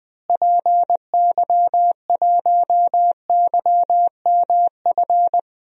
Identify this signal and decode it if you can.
Morse code, heard as a keyed tone, about 20 wpm, 700 Hz. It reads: PY1YMF
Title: DXCC-100 Call Signs - 6 Characters - Repeated 2x - 20wpm